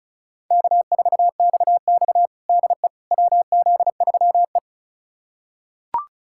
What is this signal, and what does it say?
Morse code, heard as a keyed tone, about 36 wpm, 700 Hz. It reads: K4XX DE WZ3E
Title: USA Callsign DE Callsign - 4 Characters - 35wpm